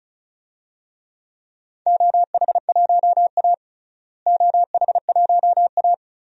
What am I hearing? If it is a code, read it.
Morse code, heard as a keyed tone, about 36 wpm, 700 Hz. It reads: OH1A OH1A
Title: DXCC-100 Call Signs - 4 Characters - Repeated 3x - 35wpm